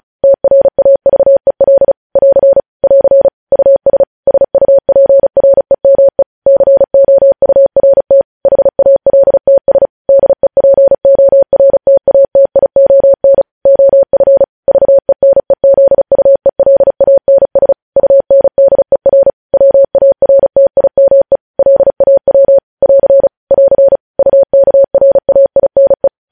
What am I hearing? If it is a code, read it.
Morse code, heard as a keyed tone, about 36 wpm, 550 Hz. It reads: TRAVEL <AR> <AR> US SUPREME COURT HALTS DEPORTATION OF VENEZUELANS UNDER WARTIME LAW <AR> <AR> UKRAINE
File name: podcast / News headlines-1x-35x35